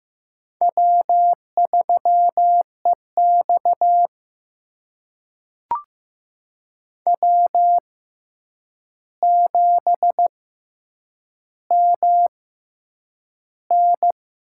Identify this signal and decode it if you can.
Morse code, heard as a keyed tone, about 15 wpm, spaced as for 8 wpm, 700 Hz. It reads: W3EX W7MN
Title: USA Call Signs - 4 Characters - 6x Character Spacing - 15wpm